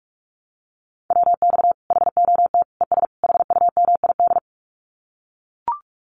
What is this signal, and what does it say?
Morse code, heard as a keyed tone, about 44 wpm, 700 Hz. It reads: WX HOT ES HUMID
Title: Common QSO Phrases 45wpm